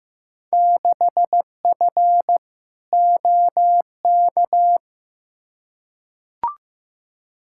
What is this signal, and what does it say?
Morse code, heard as a keyed tone, about 15 wpm, 700 Hz. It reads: 6F OK
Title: Field Day Exchanges Encore - 15wpm